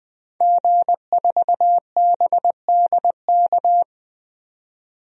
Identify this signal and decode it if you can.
Morse code, heard as a keyed tone, about 20 wpm, 700 Hz. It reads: G4BDK
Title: DXCC-100 Call Signs - 5 Characters - 20wpm